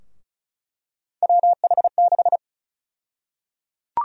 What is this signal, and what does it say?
Morse code, heard as a keyed tone, about 34 wpm, 700 Hz. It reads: WH6
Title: USA Call Sign Prefixes - 35wpm